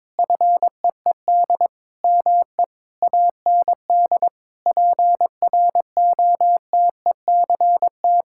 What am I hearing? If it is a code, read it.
Morse code, heard as a keyed tone, about 22 wpm, 700 Hz. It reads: FEED ME AND PROTECT
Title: Sentences from Top 1600 Words 22wpm